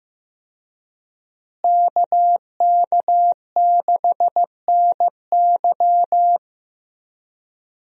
Morse code, heard as a keyed tone, about 15 wpm, 700 Hz. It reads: KK6NY
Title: USA Call Signs - 5 Characters - Repeated 2x - 15wpm